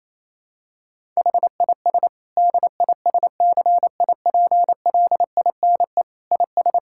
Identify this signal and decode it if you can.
Morse code, heard as a keyed tone, about 28 wpm, 700 Hz. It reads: HIS DISCIPLINE IS
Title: Sentences from Top 1000 Words - Review Part 6 - 28wpm